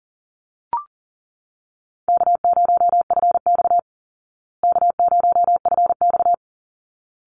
Morse code, heard as a keyed tone, about 40 wpm, 700 Hz. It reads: K0FX K0FX
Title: Easy Call Signs Repeated 2x - 40wpm